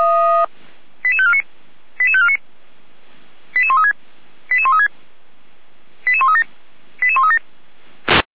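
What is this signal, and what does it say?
Five-tone selective call (ZVEI), heard as quick runs of five tones, tones of about 70 ms. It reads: 88549 88549 88136 88136 88137 88137
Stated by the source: ZVEI five-tone call 88549 received by SDR, from audio.f85360.feuerwehr-orlamuende.org fire brigade.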